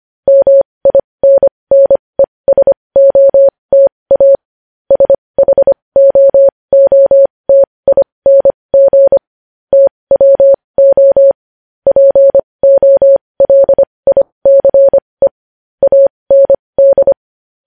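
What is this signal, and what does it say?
Morse code, heard as a keyed tone, about 25 wpm, spaced as for 20 wpm, 550 Hz. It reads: MINNESOTA SHOOTING TWO POLICE AND
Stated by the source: Morse code news headlines sent at 20 words per minute.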